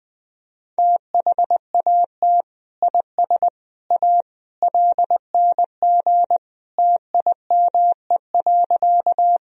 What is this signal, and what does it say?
Morse code, heard as a keyed tone, about 20 wpm, 700 Hz. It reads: THAT IS A LNG TIME.
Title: Sentences from Top 100 Words and Abbreviations 20wpm